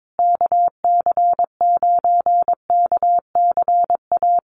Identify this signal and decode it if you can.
Morse code, heard as a keyed tone, about 22 wpm, 700 Hz. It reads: KC9KCA
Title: USA Call Signs Repeated 3x 22wpm